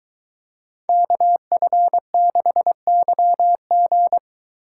Morse code, heard as a keyed tone, about 23 wpm, 700 Hz. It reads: KF6YG